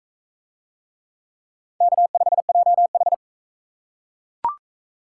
Morse code, heard as a keyed tone, about 38 wpm, 700 Hz. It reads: K5JH